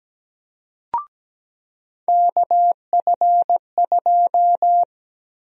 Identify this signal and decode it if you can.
Morse code, heard as a keyed tone, about 17 wpm, 700 Hz. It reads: KF2
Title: USA Call Sign Prefixes - Repeated 2x - 17wpm